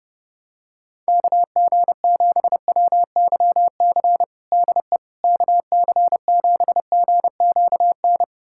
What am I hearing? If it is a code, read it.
Morse code, heard as a keyed tone, about 30 wpm, 700 Hz. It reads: KG7WYC DE KC7GQN
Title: USA Callsign DE Callsign 30wpm